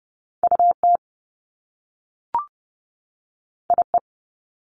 Morse code, heard as a keyed tone, about 30 wpm, 700 Hz. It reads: UT IE
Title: Top Two Letter Combinations in Words 30wpm